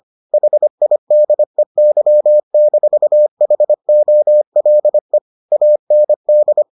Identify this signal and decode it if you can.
Morse code, heard as a keyed tone, about 25 wpm, 600 Hz. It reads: HIDEY-HOLE AND